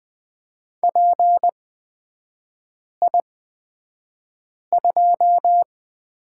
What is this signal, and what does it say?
Morse code, heard as a keyed tone, about 20 wpm, spaced as for 5 wpm, 700 Hz. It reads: PI2
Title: 236 Lesson 38e - Letter q - Three Characters 20wpm